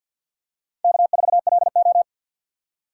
Morse code, heard as a keyed tone, about 50 wpm, 700 Hz. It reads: K4LO
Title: USA Call Signs - 2x Character Spacing - 50wpm